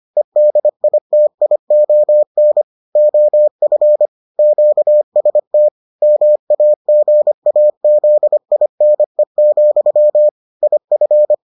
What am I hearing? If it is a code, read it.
Morse code, heard as a keyed tone, about 25 wpm, 600 Hz. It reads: EDITION OF QST MAGAZINE, IF